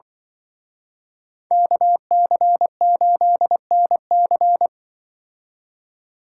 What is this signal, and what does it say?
Morse code, heard as a keyed tone, about 25 wpm, 700 Hz. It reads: KC8NC
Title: Speed Racing - USA 2x2 Call Signs 20wpm